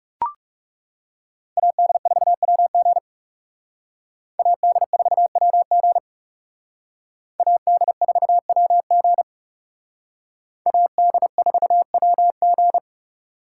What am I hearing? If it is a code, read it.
Morse code, heard as a keyed tone, about 36 wpm, 700 Hz. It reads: AD4WG AD4WG AD4WG AD4WG
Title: Speed Racing - USA 2x2 Call Signs 30wpm